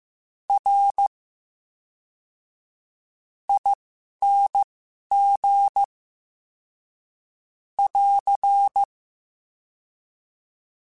Morse code, heard as a keyed tone, about 15 wpm, spaced as for 11 wpm, 800 Hz. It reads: R ING <AR>
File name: Special INGs / Special INGs-mp3s / VerySlowToFaster Ing @10